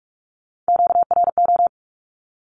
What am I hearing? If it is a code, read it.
Morse code, heard as a keyed tone, about 44 wpm, 700 Hz. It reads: QRO
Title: Q-Codes 45wpm